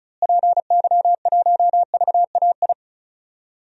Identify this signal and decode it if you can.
Morse code, heard as a keyed tone, about 36 wpm, 700 Hz. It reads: PY1VAI
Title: DXCC-100 Call Signs - 6 Characters - Repeated 2x - 35wpm